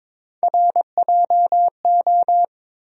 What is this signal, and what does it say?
Morse code, heard as a keyed tone, about 21 wpm, 700 Hz. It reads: RJO
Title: USA Call Sign Postfixes - 3 Letters 22wpm